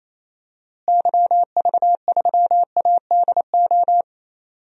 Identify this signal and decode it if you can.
Morse code, heard as a keyed tone, about 28 wpm, 700 Hz. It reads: YV3ADO
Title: DXCC-100 Call Signs - 6 Characters - Repeated 3x - 28wpm